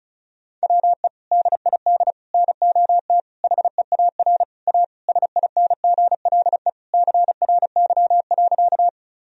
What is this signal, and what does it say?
Morse code, heard as a keyed tone, about 36 wpm, 700 Hz. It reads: WE DID NOT HEAR A SINGLE CRY.
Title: Sentences from Top 1000 Words - Review Part 4 - 35wpm